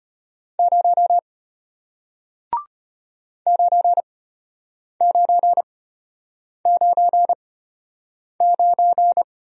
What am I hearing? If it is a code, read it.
Morse code, heard as a keyed tone, about 34 wpm, spaced as for 8 wpm, 700 Hz. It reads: 0 9999